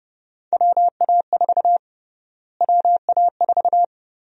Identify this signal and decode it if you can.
Morse code, heard as a keyed tone, about 30 wpm, 700 Hz. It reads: WA4 WA4
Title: USA Call Sign Prefixes - Repeated 2x - 30wpm